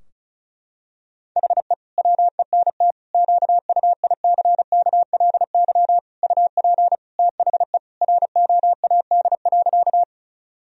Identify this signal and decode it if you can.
Morse code, heard as a keyed tone, about 34 wpm, 700 Hz. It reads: HE WENT QUICKLY UP THE ROAD.
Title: Sentences from Top 1000 Words - Review Part 2 - 35wpm